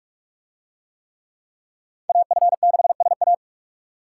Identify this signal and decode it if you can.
Morse code, heard as a keyed tone, about 40 wpm, 700 Hz. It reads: AF6SA